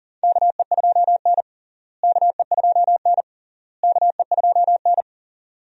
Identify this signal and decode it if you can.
Morse code, heard as a keyed tone, about 40 wpm, 700 Hz. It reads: KE2N KE2N KE2N
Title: USA Call Signs - 4 Characters - Repeated 3x - 40wpm